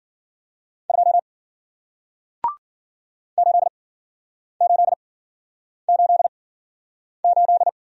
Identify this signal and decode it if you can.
Morse code, heard as a keyed tone, about 52 wpm, spaced as for 9 wpm, 700 Hz. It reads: 2 8888